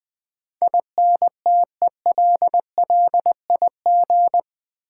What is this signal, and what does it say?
Morse code, heard as a keyed tone, about 20 wpm, 700 Hz. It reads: INTELLIG
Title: Build A Word - 4 Letters At A Time - 12-Letter Words - 20wpm